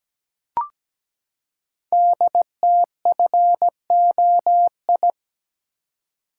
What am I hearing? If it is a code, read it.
Morse code, heard as a keyed tone, about 17 wpm, 700 Hz. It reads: DTFOI
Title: Code Groups - 5 Letters - 17wpm